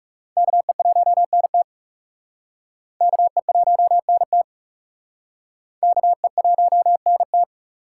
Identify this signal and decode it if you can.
Morse code, heard as a keyed tone, about 37 wpm, 700 Hz. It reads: KE1NT KE1NT KE1NT
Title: Speed Racing - USA 2x2 Call Signs 30wpm